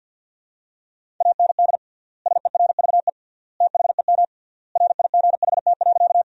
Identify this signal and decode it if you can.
Morse code, heard as a keyed tone, about 50 wpm, 700 Hz. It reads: AND SERVE THEM RIGHT.
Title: 3x Word-Spacing - Sentences from Top 400 Words - 50wpm